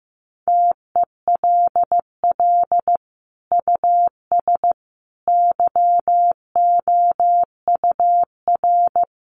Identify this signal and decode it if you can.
Morse code, heard as a keyed tone, about 15 wpm, 700 Hz. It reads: TELL US YOUR
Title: Sentences from Top 200 Words 15wpm